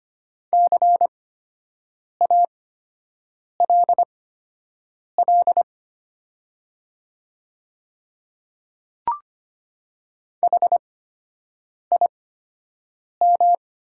Morse code, heard as a keyed tone, about 25 wpm, spaced as for 7 wpm, 700 Hz. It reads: CALL HIM
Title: 8x Character Spacing - Top 200 Words - 25wpm